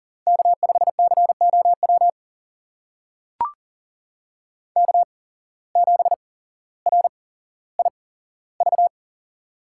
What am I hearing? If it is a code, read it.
Morse code, heard as a keyed tone, about 41 wpm, spaced as for 17 wpm, 700 Hz. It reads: K5COW K7RIV
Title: USA Call Signs - 5 Characters - 8x Character Spacing - 40wpm